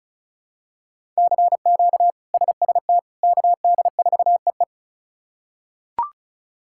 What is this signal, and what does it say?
Morse code, heard as a keyed tone, about 36 wpm, 700 Hz. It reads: CQ SST KD4EE
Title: K1USN SST QSO Practice - 35wpm